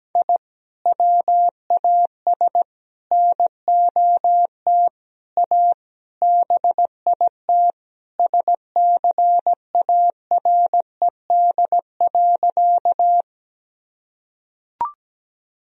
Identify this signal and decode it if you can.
Morse code, heard as a keyed tone, about 17 wpm, 700 Hz. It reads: I WAS NOT A BIT SCARED.
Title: Sentences from Top 1000 Words - Review Part 4 - 17wpm